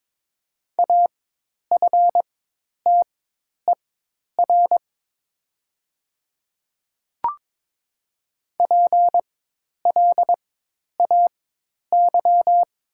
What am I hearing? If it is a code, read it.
Morse code, heard as a keyed tone, about 22 wpm, spaced as for 10 wpm, 700 Hz. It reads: AFTER PLAY
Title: 4x Character Spacing - Top 200 Words - 22wpm